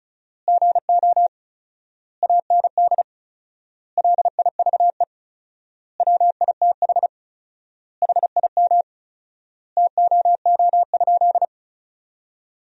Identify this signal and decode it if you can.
Morse code, heard as a keyed tone, about 36 wpm, 700 Hz. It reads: GO AND LIVE WITH HIM TOO?
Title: 4x Word-Spacing - Sentences from Top 500 Words - 35wpm